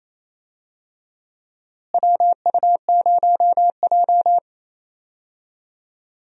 Morse code, heard as a keyed tone, about 28 wpm, 700 Hz. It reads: WU0J
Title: USA Call Signs - 4 Characters - Repeated 3x - 28wpm